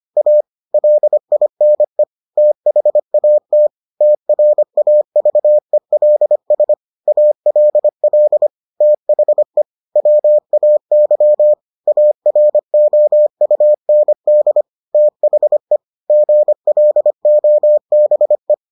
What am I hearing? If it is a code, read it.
Morse code, heard as a keyed tone, about 25 wpm, 600 Hz. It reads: A LINE THAT TRAVELS ALL THE WAY AROUND THE GLOBE